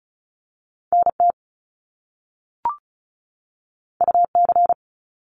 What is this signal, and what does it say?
Morse code, heard as a keyed tone, about 36 wpm, 700 Hz. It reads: NT UC
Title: Top Two Letter Combinations in Words 35wpm